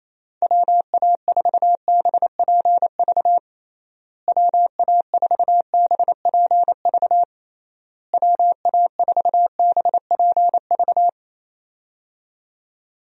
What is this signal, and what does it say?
Morse code, heard as a keyed tone, about 27 wpm, 700 Hz. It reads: WA4BPV WA4BPV WA4BPV
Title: USA Call Signs - 6 Characters - Repeated 3x - 28wpm